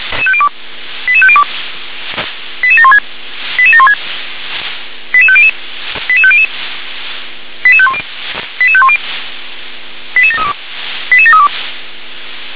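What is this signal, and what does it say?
Five-tone selective call (ZVEI), heard as quick runs of five tones, tones of about 70 ms. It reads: 88502 88716 88716 88500 88500 88510 88510 88732 88732
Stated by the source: Five-tone ZVEI selective call 88731 received on SDR.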